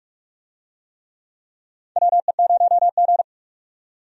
Morse code, heard as a keyed tone, about 42 wpm, 700 Hz. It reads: WE0G
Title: USA Call Signs - 4 Characters - 45wpm